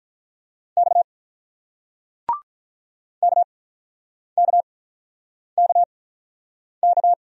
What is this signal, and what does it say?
Morse code, heard as a keyed tone, about 47 wpm, spaced as for 9 wpm, 700 Hz. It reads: X KKKK